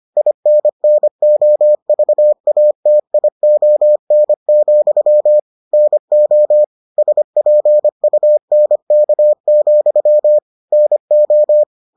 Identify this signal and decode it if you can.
Morse code, heard as a keyed tone, about 25 wpm, 600 Hz. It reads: INNOVATION, NO SPUNK, NO